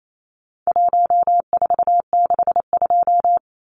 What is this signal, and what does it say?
Morse code, heard as a keyed tone, about 28 wpm, 700 Hz. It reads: 1462
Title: Numbers - 4-Digit 28wpm